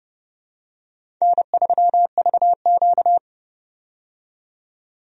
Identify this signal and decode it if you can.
Morse code, heard as a keyed tone, about 30 wpm, 700 Hz. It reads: N3VQ
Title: USA Call Signs - 4 Characters - 30wpm